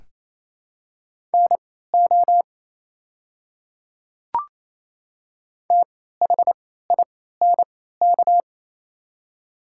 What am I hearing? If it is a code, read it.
Morse code, heard as a keyed tone, about 28 wpm, spaced as for 16 wpm, 700 Hz. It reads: NO THINK
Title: 3x Character Spacing - Top 200 Words - 28wpm